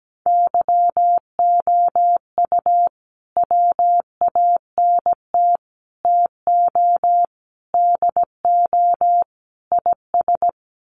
Morse code, heard as a keyed tone, about 17 wpm, 700 Hz. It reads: YOU WANT TO DO IS
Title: Sets of 5 Words Encore2 17wpm